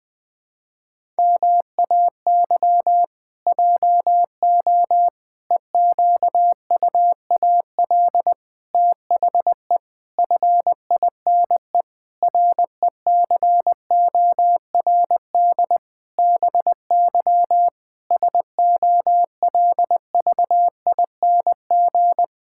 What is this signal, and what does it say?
Morse code, heard as a keyed tone, about 21 wpm, 700 Hz. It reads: MAY JO EQUAL THE FINE RECORD BY SOLVING
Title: Pangram Challenge v2 - 20wpm